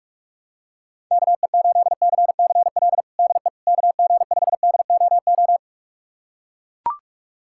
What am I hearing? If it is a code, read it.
Morse code, heard as a keyed tone, about 44 wpm, 700 Hz. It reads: KE8CCL DE KG5DOO
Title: USA Callsign DE Callsign - 6 Characters - 45wpm